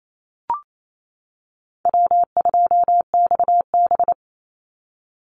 Morse code, heard as a keyed tone, about 28 wpm, 700 Hz. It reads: W2XB
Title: Easy Call Signs - 28wpm